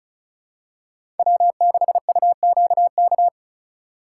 Morse code, heard as a keyed tone, about 36 wpm, 700 Hz. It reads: W6UQK
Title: USA Call Signs - 5 Characters - 3x Character Spacing - 35wpm